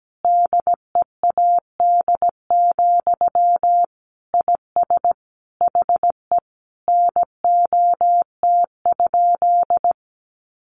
Morse code, heard as a keyed tone, about 17 wpm, 700 Hz. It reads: DEAD, IS HE NOT?
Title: Sentences from Top 1500 Words 17wpm